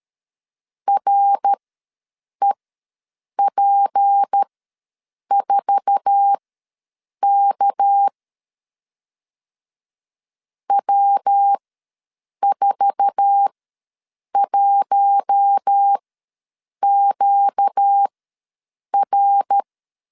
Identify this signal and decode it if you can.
Morse code, heard as a keyed tone, about 13 wpm, spaced as for 7 wpm, 800 Hz. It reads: REP4K W41QR
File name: LEZIONI / 8 YJ45 / 8 YJ4 ws 12wpm 3249